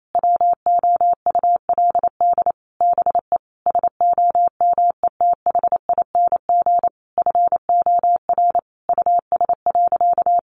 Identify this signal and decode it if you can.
Morse code, heard as a keyed tone, about 28 wpm, 700 Hz. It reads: WOULD BE SOMETHING FOR US.
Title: Sentences from Top 1200 Words 28wpm